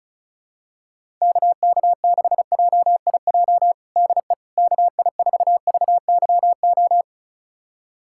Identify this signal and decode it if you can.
Morse code, heard as a keyed tone, about 34 wpm, 690 Hz. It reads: KK6JIJ DE KI4VYO
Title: USA Callsign DE Callsign - 6 Characters - 35wpm